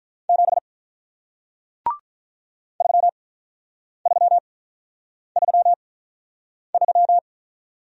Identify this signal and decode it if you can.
Morse code, heard as a keyed tone, about 48 wpm, spaced as for 9 wpm, 700 Hz. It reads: 7 3333